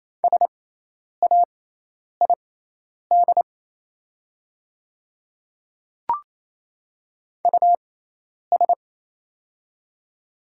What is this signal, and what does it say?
Morse code, heard as a keyed tone, about 28 wpm, spaced as for 10 wpm, 700 Hz. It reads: SAID US
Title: 6x Character Spacing - Top 200 Words - 28wpm